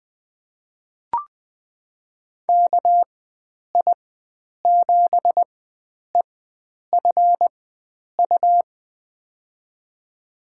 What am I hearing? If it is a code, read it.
Morse code, heard as a keyed tone, about 20 wpm, spaced as for 9 wpm, 700 Hz. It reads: KI7EFU
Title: USA Call Signs - 4x Character Spacing - 20wpm